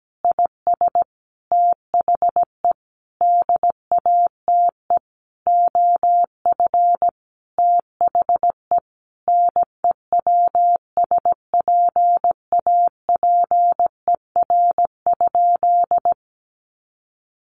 Morse code, heard as a keyed tone, about 17 wpm, 700 Hz. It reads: IS THE DATE OF THE NEWSPAPER?
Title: Sentences from Top 1100 Words 17wpm